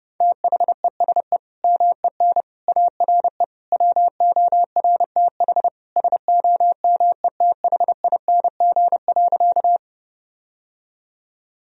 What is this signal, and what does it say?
Morse code, heard as a keyed tone, about 30 wpm, 700 Hz. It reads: THESE MEN ARE WORTH SOMETHING.
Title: Sentences from Top 1000 Words - Review Part 1 - 30wpm